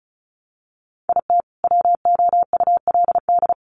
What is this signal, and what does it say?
Morse code, heard as a keyed tone, about 34 wpm, 700 Hz. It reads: IT WOULD
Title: Sentences from Top 1300 Words 35wpm